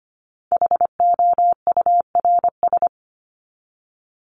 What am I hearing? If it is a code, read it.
Morse code, heard as a keyed tone, about 25 wpm, 700 Hz. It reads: HOURS
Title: ES:S Suffix - Top 100 Words with Base Words 25wpm